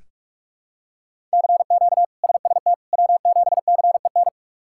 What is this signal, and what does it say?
Morse code, heard as a keyed tone, about 42 wpm, 700 Hz. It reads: CQ SST W7CEN
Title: K1USN SST QSO Practice - 45wpm